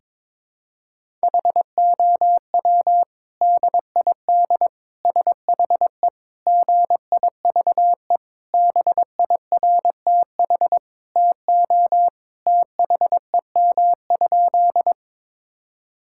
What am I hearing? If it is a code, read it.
Morse code, heard as a keyed tone, about 22 wpm, 700 Hz. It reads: HOW DID SHE GIVE BIRTH TO THEM?
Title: Sentences from Top 1000 Words - Review Part 5 - 22wpm